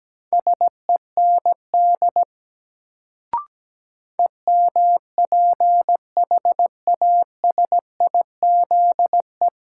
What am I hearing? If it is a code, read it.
Morse code, heard as a keyed tone, about 17 wpm, 700 Hz. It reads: SEND EMPHASIZE
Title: Sentences from Top 1500 Words - New Word Drill Alt - 17wpm